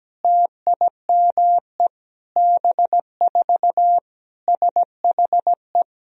Morse code, heard as a keyed tone, about 17 wpm, 700 Hz. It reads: TIME B4 SHE
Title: Sentences from Top 500 Words and Abbreviations 17wpm